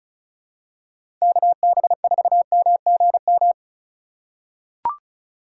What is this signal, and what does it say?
Morse code, heard as a keyed tone, about 36 wpm, 700 Hz. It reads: KB4MGM
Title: USA Call Signs - 2x Character Spacing - 35wpm